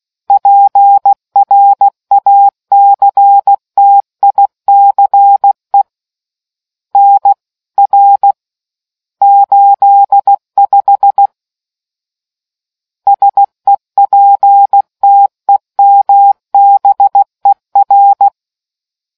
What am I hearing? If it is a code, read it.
Morse code, heard as a keyed tone, about 16 wpm, 800 Hz. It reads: PRACTICE NR 85 SEPTEMBER